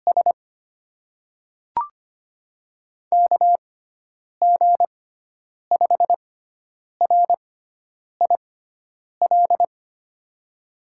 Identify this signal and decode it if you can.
Morse code, heard as a keyed tone, about 25 wpm, spaced as for 9 wpm, 700 Hz. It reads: S KG5RIL
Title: USA Call Signs - 6 Characters - 6x Character Spacing - 25wpm